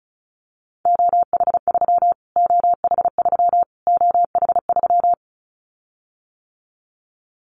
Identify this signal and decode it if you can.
Morse code, heard as a keyed tone, about 34 wpm, 700 Hz. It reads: OH3 OH3 OH3
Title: Callsign Prefixes for Top 25 DXCC - Repeated 3x - 35wpm